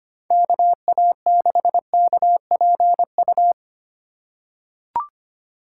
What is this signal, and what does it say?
Morse code, heard as a keyed tone, about 25 wpm, 700 Hz. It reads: KA6KPU
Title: USA Call Signs - 6 Characters - 3x Character Spacing - 25wpm